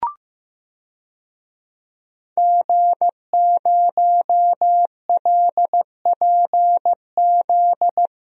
Morse code, heard as a keyed tone, about 15 wpm, 700 Hz. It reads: G0LPZ
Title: DXCC-100 Call Signs - 5 Characters - Repeated 3x - 15wpm